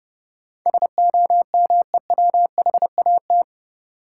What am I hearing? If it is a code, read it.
Morse code, heard as a keyed tone, about 30 wpm, 700 Hz. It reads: SOMEWHAT
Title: Sentences from Top 1000 Words - Review Part 1 - New Word Drill - 30wpm